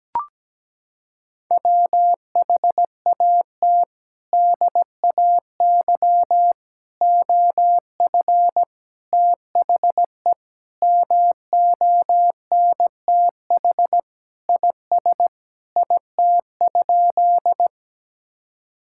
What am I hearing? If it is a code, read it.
Morse code, heard as a keyed tone, about 17 wpm, 700 Hz. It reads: WHAT DAY OF THE MONTH IS IT?
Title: Sentences from Top 1000 Words - Review Part 5 - 17wpm